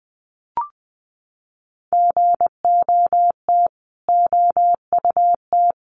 Morse code, heard as a keyed tone, about 20 wpm, 700 Hz. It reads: GOT OUT
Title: Sets of 5 Words Encore 20wpm